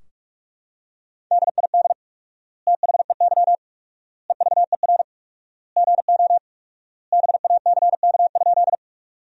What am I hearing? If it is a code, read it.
Morse code, heard as a keyed tone, about 46 wpm, 700 Hz. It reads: DID THEY EVER GO BACK?
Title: 4x Word-Spacing - Sentences from Top 400 Words - 45wpm